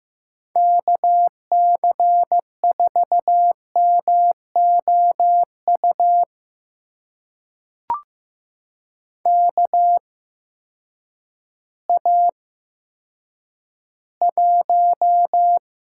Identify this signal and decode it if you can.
Morse code, heard as a keyed tone, about 15 wpm, spaced as for 9 wpm, 700 Hz. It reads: KC4MOU KA1
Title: USA Call Signs - 6 Characters - 8x Character Spacing - 15wpm